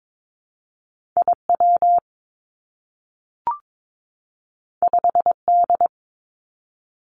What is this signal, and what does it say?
Morse code, heard as a keyed tone, about 22 wpm, 700 Hz. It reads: I W 5 D
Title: Two Challenging Letter-Number Combinations 22wpm